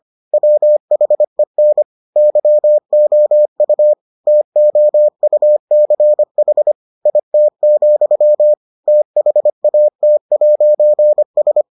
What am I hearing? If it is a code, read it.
Morse code, heard as a keyed tone, about 25 wpm, 600 Hz. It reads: WHEN YOU TOUCH IT, THAT'S